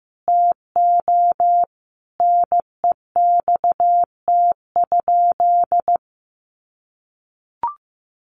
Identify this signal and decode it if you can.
Morse code, heard as a keyed tone, about 15 wpm, 700 Hz. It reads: TO NEXT?
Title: Sentences from Top 1200 Words 15wpm